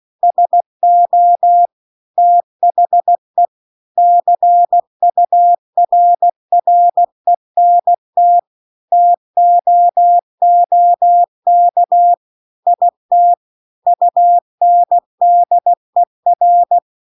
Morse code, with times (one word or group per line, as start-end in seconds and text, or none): 0.2-1.7 SO
2.2-3.5 THE
4.0-8.4 CURRENT
8.9-12.1 TOOK
12.7-13.3 IT
13.9-16.8 UNDER